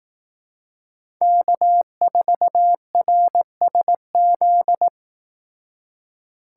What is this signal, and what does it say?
Morse code, heard as a keyed tone, about 18 wpm, 700 Hz. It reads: K4RSZ
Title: Speed Racing - USA 1x3 Call Signs 15wpm